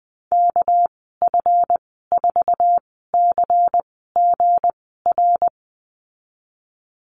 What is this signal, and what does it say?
Morse code, heard as a keyed tone, about 19 wpm, spaced as for 14 wpm, 700 Hz. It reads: KF4CGR
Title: 238 Lesson 38g - Letter q - Call Signs 20wpm